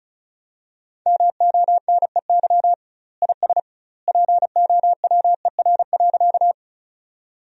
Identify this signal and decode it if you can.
Morse code, heard as a keyed tone, about 36 wpm, 700 Hz. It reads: MONEY IS POWER.
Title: 2x Word-Spacing - Sentences from Top 500 Words - 35wpm